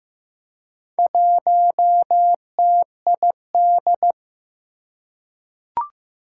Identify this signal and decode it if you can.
Morse code, heard as a keyed tone, about 15 wpm, 700 Hz. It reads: 1TID
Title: Four Challenging Letter-Number Combinations 15wpm